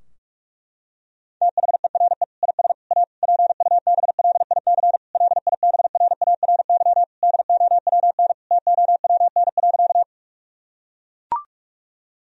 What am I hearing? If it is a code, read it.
Morse code, heard as a keyed tone, about 41 wpm, 700 Hz. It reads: THERE IS A PUBLIC LIBRARY DOWN TOWN.
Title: Sentences from Top 1000 Words - Review Part 4 - 45wpm